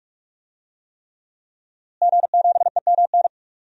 Morse code, heard as a keyed tone, about 46 wpm, 700 Hz. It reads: G7EMN
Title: DXCC-100 Call Signs - 5 Characters - Repeated 3x - 45wpm